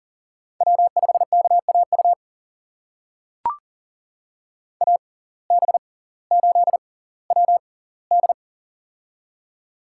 Morse code, heard as a keyed tone, about 40 wpm, spaced as for 20 wpm, 700 Hz. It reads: W5KAU AB8WD
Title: USA Call Signs - 6x Character Spacing - 40wpm